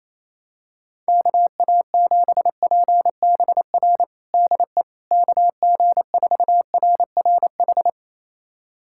Morse code, heard as a keyed tone, about 28 wpm, 700 Hz. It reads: KA7PBR DE KG4RRH
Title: USA Callsign DE Callsign - 6 Characters - 28wpm